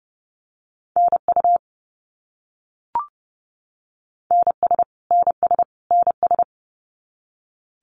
Morse code, heard as a keyed tone, about 30 wpm, 700 Hz. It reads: NU NS NS NS
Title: Canadian Provinces and Territories - Repeated 3x - 30wpm